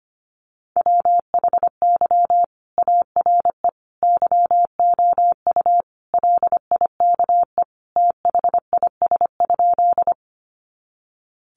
Morse code, heard as a keyed tone, about 25 wpm, 700 Hz. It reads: WHY ARE YOU LIKE THIS?
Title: Sentences from Top 200 Words 25wpm